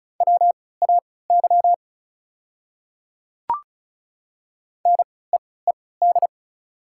Morse code, heard as a keyed tone, about 34 wpm, spaced as for 20 wpm, 700 Hz. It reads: WAY NEED
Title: 3x Character Spacing - Top 200 Words - 35wpm